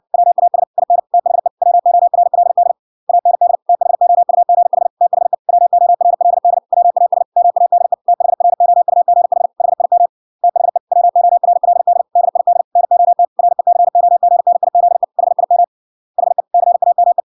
Morse code, heard as a keyed tone, about 61 wpm, 700 Hz. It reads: WAS IN THE WORLD AND THOUGH THE WORLD WAS MADE THROUGH HIM THE WORLD DID NOT RECOGNIZE HIM HE CAME